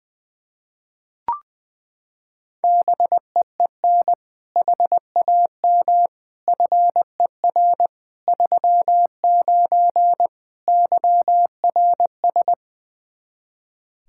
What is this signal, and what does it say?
Morse code, heard as a keyed tone, about 20 wpm, 700 Hz. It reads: BEEN HAM FER 39 YRS
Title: K1USN SST QSO Practice 20wpm